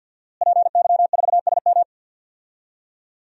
Morse code, heard as a keyed tone, about 51 wpm, 700 Hz. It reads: PY4SM
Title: DXCC-100 Call Signs - 5 Characters - Repeated 3x - 50wpm